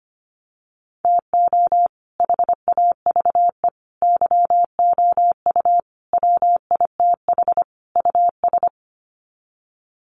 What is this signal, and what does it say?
Morse code, heard as a keyed tone, about 25 wpm, 700 Hz. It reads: TO HAVE YOU WITH US
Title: Sets of 5 Words 25wpm